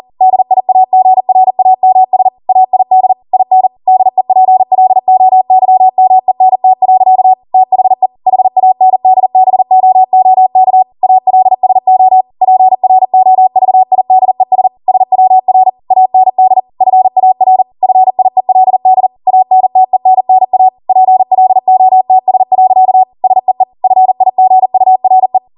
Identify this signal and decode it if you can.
Morse code, heard as a keyed tone, about 40 wpm, 750 Hz. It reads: DIAGRAMS AID IN DEPLOYMENT. THE HANDBOOK ALSO PROVIDES SWR AND FAR FIELD ANTENNA PLOTS. SEE FIGURE